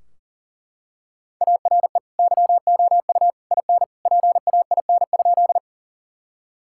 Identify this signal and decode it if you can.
Morse code, heard as a keyed tone, about 37 wpm, 700 Hz. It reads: ARE YOU IN PAIN?
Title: Sentences from Top 1600 Words 40wpm